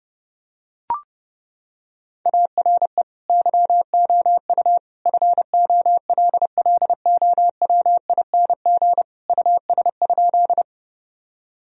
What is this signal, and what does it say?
Morse code, heard as a keyed tone, about 30 wpm, 700 Hz. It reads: ARE YOU FOLLOWING US?
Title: Sentences from Top 1000 Words - Review Part 4 - 30wpm